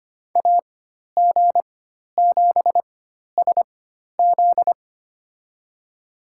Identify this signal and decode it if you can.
Morse code, heard as a keyed tone, about 25 wpm, spaced as for 12 wpm, 700 Hz. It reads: AG7SZ
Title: USA Call Signs - 5 Characters - 4x Character Spacing - 25wpm